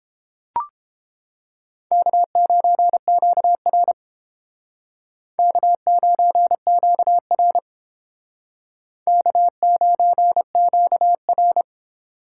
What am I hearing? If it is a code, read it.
Morse code, heard as a keyed tone, about 30 wpm, 700 Hz. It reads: K9QR K9QR K9QR